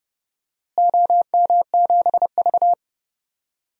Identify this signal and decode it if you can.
Morse code, heard as a keyed tone, about 31 wpm, 700 Hz. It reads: OM7V
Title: DXCC-100 Call Signs - 4 Characters - Repeated 3x - 30wpm